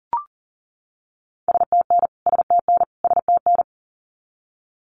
Morse code, heard as a keyed tone, about 40 wpm, 700 Hz. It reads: STN STN STN
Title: Abbreviations Repeated 3x - 40wpm